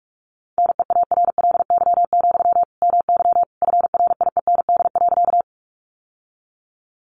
Morse code, heard as a keyed tone, about 44 wpm, 700 Hz. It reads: NEARLY, MY FRIEND.
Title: Sentences from Top 1300 Words 45wpm